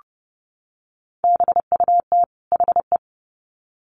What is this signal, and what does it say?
Morse code, heard as a keyed tone, about 30 wpm, 700 Hz. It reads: BUT HE
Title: Sets of 2 Words 30wpm